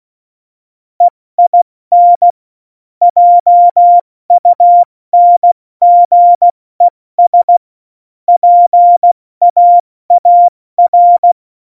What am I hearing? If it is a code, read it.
Morse code, heard as a keyed tone, about 16 wpm, 700 Hz. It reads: EIN JUNGES PAAR
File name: Morsekurs / Witze / Witz 04. Juni 2019, 70 F80